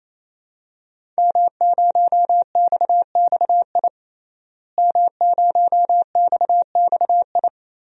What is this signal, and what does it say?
Morse code, heard as a keyed tone, about 28 wpm, 700 Hz. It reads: M0XXI M0XXI
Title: DXCC-100 Call Signs - 5 Characters - Repeated 3x - 28wpm